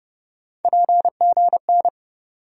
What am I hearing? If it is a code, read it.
Morse code, heard as a keyed tone, about 30 wpm, 700 Hz. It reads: PGN
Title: USA Call Sign Postfixes - 3 Letters 30wpm